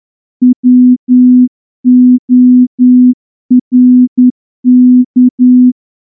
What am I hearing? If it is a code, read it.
Morse code, heard as a keyed tone, about 9 wpm, 250 Hz. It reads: WORK
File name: morse-training-10WPM